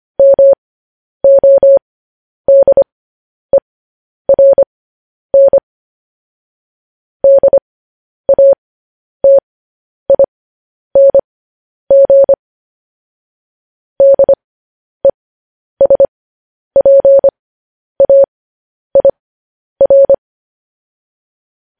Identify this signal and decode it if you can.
Morse code, heard as a keyed tone, about 25 wpm, spaced as for 10 wpm, 550 Hz. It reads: MODERN DATING DESPAIR